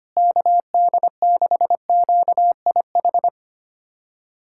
Morse code, heard as a keyed tone, about 25 wpm, 700 Hz. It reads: KD6QIH
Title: USA Callsign DE Callsign - 6 Characters - 25wpm